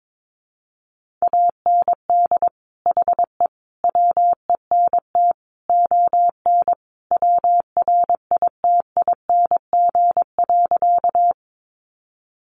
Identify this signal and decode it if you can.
Morse code, heard as a keyed tone, about 22 wpm, 700 Hz. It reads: AND HE WENT ON WRITING.
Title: Sentences from Top 800 Words 22wpm